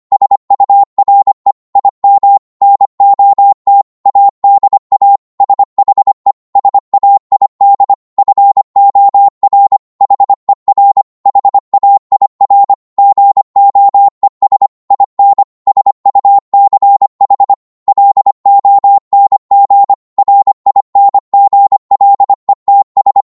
Morse code, heard as a keyed tone, about 25 wpm, 800 Hz. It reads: SURE IM NOT ADA SHE SAID FOR HER HAIR GOES IN SUCH LONG RINGLETS